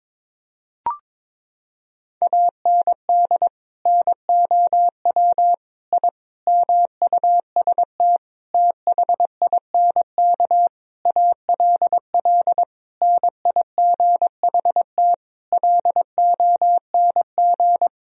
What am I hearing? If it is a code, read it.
Morse code, heard as a keyed tone, about 22 wpm, 700 Hz. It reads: AND NOW I MUST THINK ALL NIGHT LONG
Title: Sentences from Top 1000 Words - Review Part 3 - 22wpm